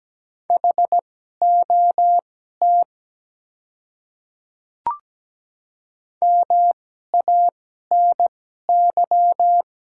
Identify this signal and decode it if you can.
Morse code, heard as a keyed tone, about 17 wpm, spaced as for 12 wpm, 700 Hz. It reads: HOT MANY
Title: 2x Character Spacing - Top 100 Words - 17wpm